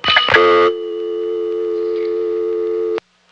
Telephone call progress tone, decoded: dial tone